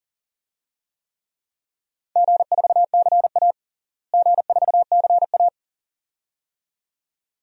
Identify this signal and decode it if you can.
Morse code, heard as a keyed tone, about 37 wpm, 700 Hz. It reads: G4CA G4CA
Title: DXCC-100 Call Signs - 4 Characters - Repeated 2x - 40wpm